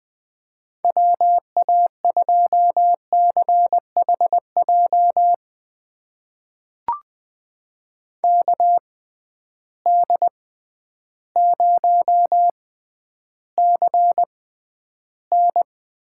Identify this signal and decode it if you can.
Morse code, heard as a keyed tone, about 21 wpm, spaced as for 11 wpm, 700 Hz. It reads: WA2CHJ KD0CN
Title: USA Call Signs - 6 Characters - 6x Character Spacing - 20wpm